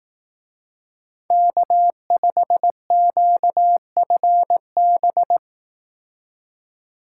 Morse code, heard as a keyed tone, about 18 wpm, 700 Hz. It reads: K5QFB
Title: Speed Racing - USA Call Signs 15wpm